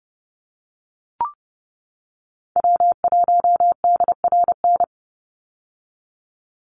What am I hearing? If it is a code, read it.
Morse code, heard as a keyed tone, about 30 wpm, 700 Hz. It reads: W1DRN
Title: USA Call Signs 30wpm